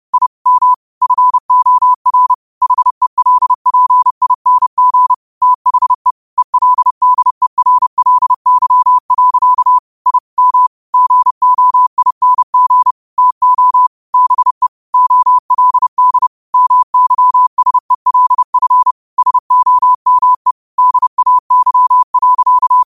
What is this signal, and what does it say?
Morse code, heard as a keyed tone, about 30 wpm, 1000 Hz. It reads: I M FOR HELPING THE ELDERLY. I M GOING TO BE OLD MYSELF SOME DAY.
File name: qotdfiles / 30wpm